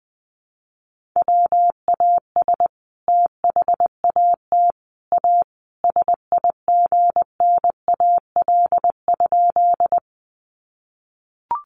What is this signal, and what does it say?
Morse code, heard as a keyed tone, about 20 wpm, 700 Hz. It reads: WAS THAT A SIGNAL?
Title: Sentences from Top 1400 Words 20wpm